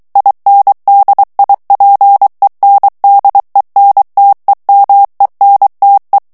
Morse code, heard as a keyed tone, about 23 wpm, 800 Hz. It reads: INDIPENDENTEMENTE
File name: ITALIEN texte clair 2 vitesse 22 mots minute